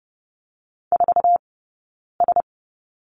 Morse code, heard as a keyed tone, about 30 wpm, spaced as for 9 wpm, 690 Hz. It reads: 4S
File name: course / 30:10wpm / 086 Lesson 15d - Number 5 - Two Characters 30wpm